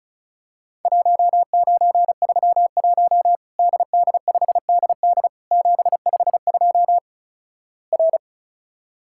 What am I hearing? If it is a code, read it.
Morse code, heard as a keyed tone, about 36 wpm, 700 Hz. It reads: 1931 DD5DD 752 R
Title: Worked All Europe DX Contest - QTC Exchange - 35wpm